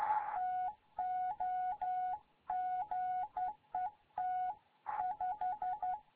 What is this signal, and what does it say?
Morse code, heard as a keyed tone, about 11 wpm, 740 Hz. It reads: TOGET 5
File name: Morze-HA2MN